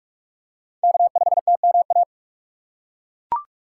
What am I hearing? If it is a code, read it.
Morse code, heard as a keyed tone, about 42 wpm, 700 Hz. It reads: K5TMA
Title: USA Call Signs - 5 Characters - 6x Character Spacing - 45wpm